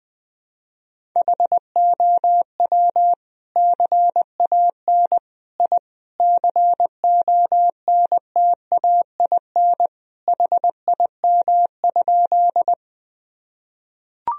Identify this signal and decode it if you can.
Morse code, heard as a keyed tone, about 20 wpm, 700 Hz. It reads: HOW CAN I CONTAIN HIM?
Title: Sentences from Top 1000 Words - Review Part 2 - 20wpm